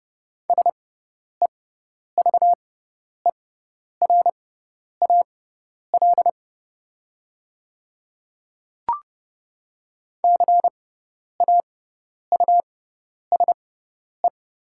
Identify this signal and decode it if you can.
Morse code, heard as a keyed tone, about 30 wpm, spaced as for 10 wpm, 690 Hz. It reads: SEVERAL CAUSE
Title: 6x Character Spacing - Top 500 Words - 30wpm